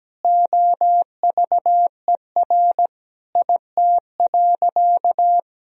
Morse code, heard as a keyed tone, about 17 wpm, 700 Hz. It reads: OVER IT.
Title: Sentences from Top 1000 Words - Review Part 3 - 17wpm